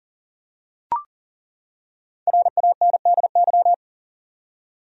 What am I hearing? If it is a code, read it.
Morse code, heard as a keyed tone, about 40 wpm, 700 Hz. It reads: RANDY
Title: USA Common Names - 40wpm